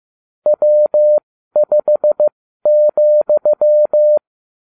Morse code, heard as a keyed tone, about 15 wpm, spaced as for 12 wpm, 600 Hz. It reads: W5,